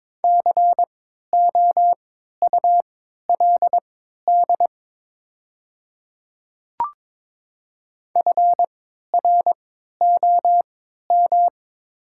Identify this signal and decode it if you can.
Morse code, heard as a keyed tone, about 22 wpm, spaced as for 12 wpm, 690 Hz. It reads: COULD FROM
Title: 3x Character Spacing - Top 100 Words - 22wpm